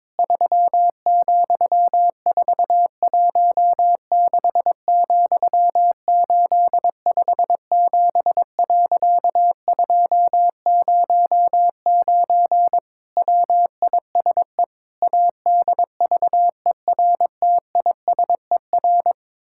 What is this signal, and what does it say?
Morse code, heard as a keyed tone, about 22 wpm, 700 Hz. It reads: 3,416,857.209 WISE ADVERTISER
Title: Pangram Challenge v2 - 22wpm